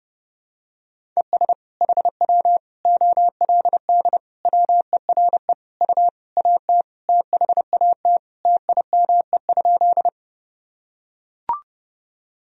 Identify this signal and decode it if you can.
Morse code, heard as a keyed tone, about 30 wpm, 700 Hz. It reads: ES HW OLD WERE U AT THAT TIME?
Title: Sentences from Top 200 Words and Abbreviations 30wpm